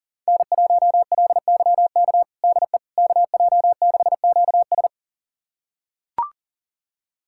Morse code, heard as a keyed tone, about 37 wpm, 700 Hz. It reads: N1LYK DE KJ6QS
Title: USA Callsign DE Callsign - 5 Characters - 40wpm